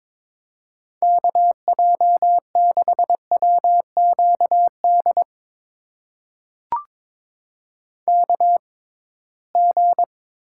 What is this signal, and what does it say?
Morse code, heard as a keyed tone, about 22 wpm, spaced as for 17 wpm, 700 Hz. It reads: KJ6WQD KG
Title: USA Call Signs - 6 Characters - 6x Character Spacing - 22wpm